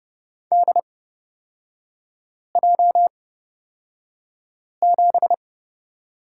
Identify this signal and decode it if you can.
Morse code, heard as a keyed tone, about 30 wpm, spaced as for 5 wpm, 700 Hz. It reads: DJ7
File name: Lesson 36e - BT - Three Characters 30wpm